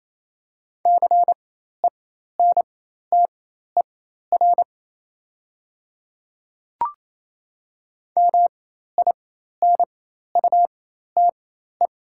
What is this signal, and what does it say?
Morse code, heard as a keyed tone, about 28 wpm, spaced as for 13 wpm, 700 Hz. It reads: CENTER MINUTE
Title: 4x Character Spacing - Top 500 Words - 28wpm